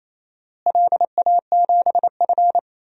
Morse code, heard as a keyed tone, about 28 wpm, 700 Hz. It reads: LA7F
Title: DXCC-100 Call Signs - 4 Characters - Repeated 2x - 28wpm